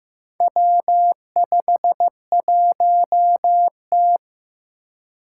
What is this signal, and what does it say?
Morse code, heard as a keyed tone, about 15 wpm, 700 Hz. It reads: W51T
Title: Four Challenging Letter-Number Combinations 15wpm